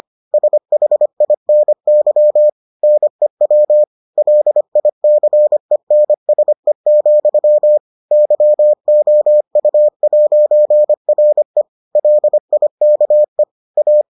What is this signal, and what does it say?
Morse code, heard as a keyed tone, about 25 wpm, 600 Hz. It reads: SHINY NEW LICENSE, YOU'RE LIKE A